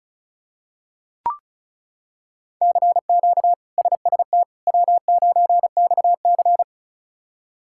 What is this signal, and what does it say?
Morse code, heard as a keyed tone, about 34 wpm, 700 Hz. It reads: CQ SST W9XC
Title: K1USN SST QSO Practice - 35wpm